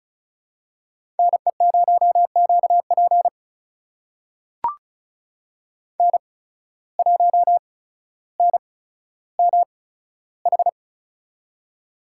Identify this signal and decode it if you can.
Morse code, heard as a keyed tone, about 34 wpm, spaced as for 15 wpm, 700 Hz. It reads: NE0QP N1NMH
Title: USA Call Signs - 5 Characters - 8x Character Spacing - 35wpm